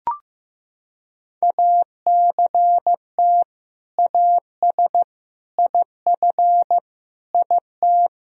Morse code, heard as a keyed tone, about 15 wpm, 700 Hz. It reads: ACT AS IF IT
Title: Sentences from Top 1000 Words - Review Part 1 - 15wpm